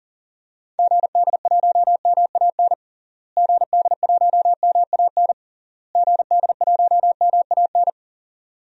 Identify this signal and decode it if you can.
Morse code, heard as a keyed tone, about 40 wpm, 700 Hz. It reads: GD1MAN GD1MAN GD1MAN
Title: DXCC-100 Call Signs - 6 Characters - Repeated 3x - 40wpm